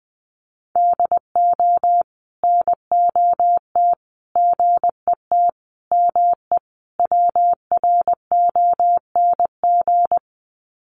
Morse code, heard as a keyed tone, about 20 wpm, 700 Hz. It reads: DO NOT GET ME WRONG
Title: Sets of 5 Words Encore 20wpm